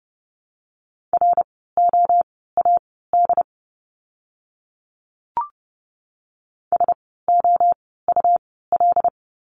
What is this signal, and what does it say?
Morse code, heard as a keyed tone, about 30 wpm, spaced as for 17 wpm, 700 Hz. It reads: ROAD SOUL
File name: Lesson 17f - Letter u - Words 30wpm